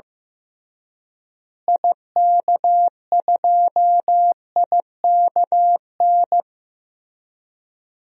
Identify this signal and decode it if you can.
Morse code, heard as a keyed tone, about 16 wpm, 700 Hz. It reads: IK2IKN
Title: DXCC-100 Call Signs - 6 Characters - Repeated 3x - 15wpm